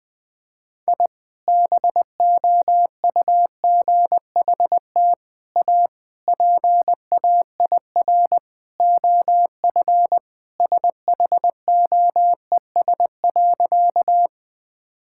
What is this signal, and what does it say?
Morse code, heard as a keyed tone, about 20 wpm, 700 Hz. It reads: I BOUGHT A PAIR OF SHOES.
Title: Sentences with 3-Word Phrases - Spoken Beginning - 20wpm